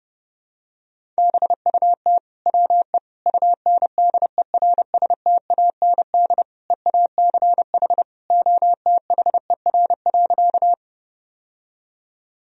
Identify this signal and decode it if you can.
Morse code, heard as a keyed tone, about 30 wpm, 700 Hz. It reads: BUT WE UNDERSTAND EACH OTHER.
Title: Sentences from Top 1000 Words - Review Part 1 - 30wpm